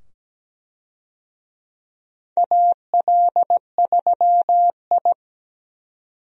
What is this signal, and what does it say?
Morse code, heard as a keyed tone, about 17 wpm, 700 Hz. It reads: AL3I
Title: USA Call Signs - 4 Characters - 17wpm